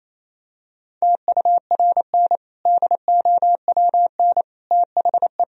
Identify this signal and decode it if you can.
Morse code, heard as a keyed tone, about 28 wpm, 700 Hz. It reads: TURN DOWN THE
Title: Sentences from Top 1000 Words - Review Part 3 - 28wpm